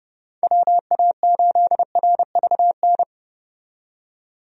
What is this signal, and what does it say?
Morse code, heard as a keyed tone, about 30 wpm, 700 Hz. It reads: WA8RVN
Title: USA Call Signs - 6 Characters - 30wpm